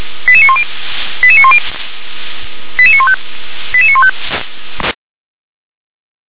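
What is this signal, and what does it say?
Five-tone selective call (ZVEI), heard as quick runs of five tones, tones of about 70 ms. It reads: 88910 88910 88915 88915
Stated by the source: Five-tone ZVEI selective call 88786 received on SDR.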